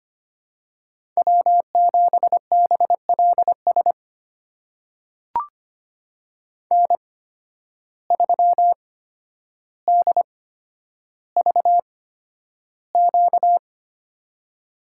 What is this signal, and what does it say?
Morse code, heard as a keyed tone, about 25 wpm, spaced as for 11 wpm, 700 Hz. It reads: W7BLS N3DVQ
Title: USA Call Signs - 5 Characters - 8x Character Spacing - 25wpm